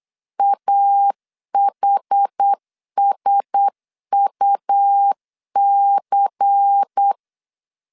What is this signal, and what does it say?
Morse code, heard as a keyed tone, about 8 wpm, 800 Hz. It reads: AHSUC
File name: LEZIONI / 6 PKX0 / 6 P 8wpm 3025